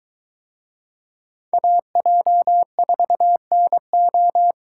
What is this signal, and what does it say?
Morse code, heard as a keyed tone, about 23 wpm, 700 Hz. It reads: AJ4NO